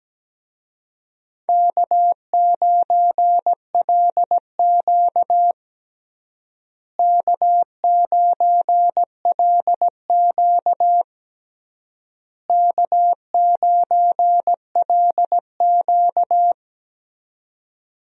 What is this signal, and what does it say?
Morse code, heard as a keyed tone, about 17 wpm, 700 Hz. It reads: K9LQ K9LQ K9LQ
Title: USA Call Signs - 4 Characters - Repeated 3x - 17wpm